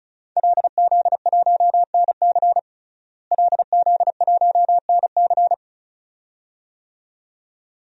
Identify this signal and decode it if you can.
Morse code, heard as a keyed tone, about 34 wpm, 700 Hz. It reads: LZ1NC LZ1NC
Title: DXCC-100 Call Signs - 5 Characters - Repeated 2x - 35wpm